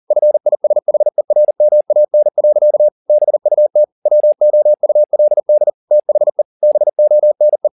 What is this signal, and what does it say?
Morse code, heard as a keyed tone, about 40 wpm, 600 Hz. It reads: FISHERMAN. BUT WOULD THE BONE